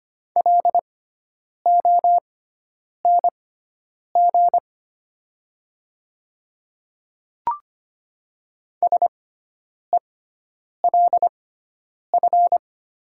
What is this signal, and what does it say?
Morse code, heard as a keyed tone, about 25 wpm, spaced as for 9 wpm, 700 Hz. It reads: LONG SELF
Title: 6x Character Spacing - Top 200 Words - 25wpm